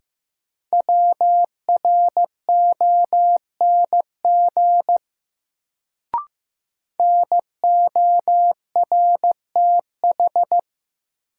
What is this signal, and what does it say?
Morse code, heard as a keyed tone, about 15 wpm, 700 Hz. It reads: WRONG NORTH
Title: Sets of 5 Words Encore - New Word Drill - 15wpm